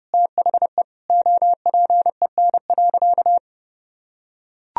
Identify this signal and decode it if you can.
Morse code, heard as a keyed tone, about 30 wpm, 700 Hz. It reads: THE OPEN.
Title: Sentences from Top 1000 Words - Review Part 1 - 30wpm